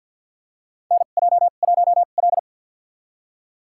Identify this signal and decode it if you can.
Morse code, heard as a keyed tone, about 53 wpm, spaced as for 37 wpm, 700 Hz. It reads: NJ1L
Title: USA Call Signs - 4 Characters - 2x Character Spacing - 50wpm